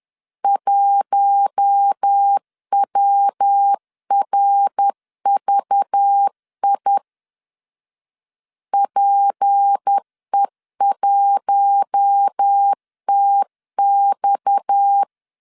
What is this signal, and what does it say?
Morse code, heard as a keyed tone, about 10 wpm, 800 Hz. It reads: 1WRVI PE1TX
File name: LEZIONI / 6 PKX0 / 6 PKX 10wpm 2426